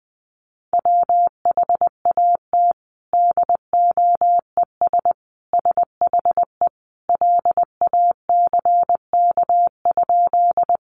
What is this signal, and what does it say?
Morse code, heard as a keyed tone, about 20 wpm, 700 Hz. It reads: WHAT DOES SHE LACK?
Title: Sentences from Top 1100 Words 20wpm